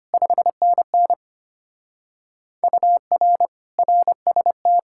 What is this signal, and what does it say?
Morse code, heard as a keyed tone, about 32 wpm, 700 Hz. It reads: 5NN UR RST